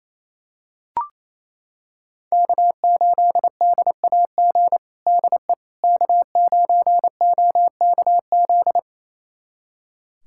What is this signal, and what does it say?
Morse code, heard as a keyed tone, about 28 wpm, 700 Hz. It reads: K8DAG DE K9OKZ
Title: USA Callsign DE Callsign - 5 Characters - 28wpm